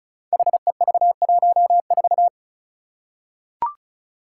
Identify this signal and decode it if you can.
Morse code, heard as a keyed tone, about 36 wpm, 700 Hz. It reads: HEV14
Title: Five Challenging Letter-Number Combinations 35wpm